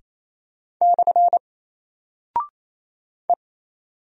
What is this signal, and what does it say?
Morse code, heard as a keyed tone, about 28 wpm, 700 Hz. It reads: / E
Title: Single Character (Letter:Number:Punctuation:Prosign) Alt 28wpm